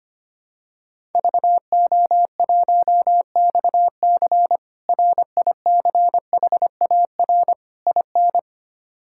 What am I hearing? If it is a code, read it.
Morse code, heard as a keyed tone, about 25 wpm, 700 Hz. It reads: VO1XC RICHAR IN
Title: K1USN SST QSO Practice - 25wpm